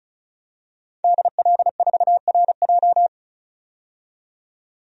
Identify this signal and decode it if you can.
Morse code, heard as a keyed tone, about 36 wpm, 700 Hz. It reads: DL4RJ
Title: DXCC-100 Call Signs - 5 Characters - 35wpm